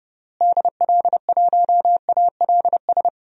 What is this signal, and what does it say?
Morse code, heard as a keyed tone, about 30 wpm, 700 Hz. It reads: DL1ALS
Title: DXCC-100 Call Signs - 6 Characters - Repeated 3x - 30wpm